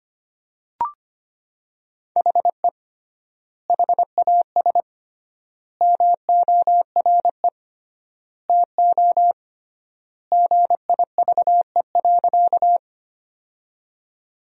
Morse code, heard as a keyed tone, about 25 wpm, 700 Hz. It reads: HE HAS MORE TO GIVE.
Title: 3x Word-Spacing - Sentences from Top 200 Words - 25wpm